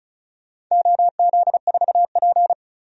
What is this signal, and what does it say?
Morse code, heard as a keyed tone, about 35 wpm, 700 Hz. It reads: OZ4P
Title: DXCC-100 Call Signs - 4 Characters - 35wpm